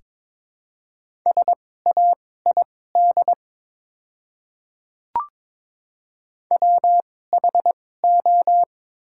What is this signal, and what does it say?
Morse code, heard as a keyed tone, about 22 wpm, spaced as for 16 wpm, 690 Hz. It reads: SAID WHO
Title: 2x Character Spacing - Top 100 Words - 22wpm